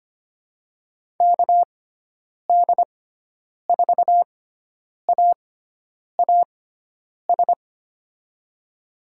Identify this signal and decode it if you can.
Morse code, heard as a keyed tone, about 25 wpm, spaced as for 9 wpm, 700 Hz. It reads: KD4AAS
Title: USA Call Signs - 6 Characters - 6x Character Spacing - 25wpm